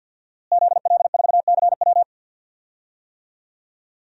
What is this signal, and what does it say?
Morse code, heard as a keyed tone, about 46 wpm, 700 Hz. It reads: ZL4CW
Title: DXCC-100 Call Signs - 5 Characters - Repeated 2x - 50wpm